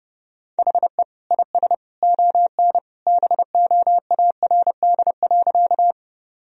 Morse code, heard as a keyed tone, about 32 wpm, 700 Hz. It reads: HE IS ON BOARD.
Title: Sentences from Top 1000 Words - Review Part 4 - 30wpm